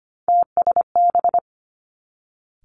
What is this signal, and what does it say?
Morse code, heard as a keyed tone, about 25 wpm, 700 Hz. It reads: TSB
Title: Three Challenging Letter-Number Combinations 25wpm